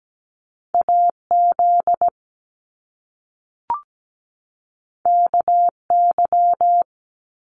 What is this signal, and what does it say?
Morse code, heard as a keyed tone, about 17 wpm, 700 Hz. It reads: AZ KY
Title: US States 17wpm